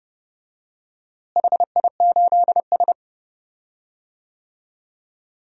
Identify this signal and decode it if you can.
Morse code, heard as a keyed tone, about 30 wpm, 700 Hz. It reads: HI8S
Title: DXCC-100 Call Signs - 4 Characters - Repeated 3x - 30wpm